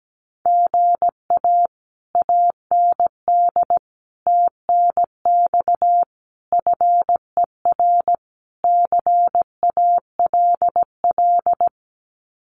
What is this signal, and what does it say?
Morse code, heard as a keyed tone, about 17 wpm, 700 Hz. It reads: GA AND TNX FER CALL
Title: Common QSO Phrases Encore 17wpm